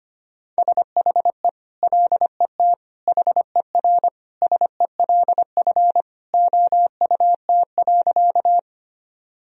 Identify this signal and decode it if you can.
Morse code, heard as a keyed tone, about 25 wpm, 700 Hz. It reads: SHE LET HERSELF OUT.
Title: Sentences from Top 1000 Words - Review Part 1 - 25wpm